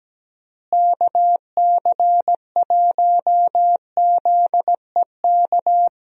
Morse code, heard as a keyed tone, about 17 wpm, 700 Hz. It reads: KC1ZEK
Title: USA Call Signs - 6 Characters - Repeated 3x - 17wpm